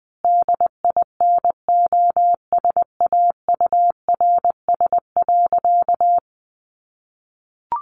Morse code, heard as a keyed tone, about 19 wpm, 700 Hz. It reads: DINOSAURS.
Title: A Sound Of Thunder 20wpm